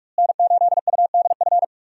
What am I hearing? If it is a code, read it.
Morse code, heard as a keyed tone, about 44 wpm, 700 Hz. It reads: N8UDF
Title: USA Call Signs - 5 Characters - Repeated 2x - 45wpm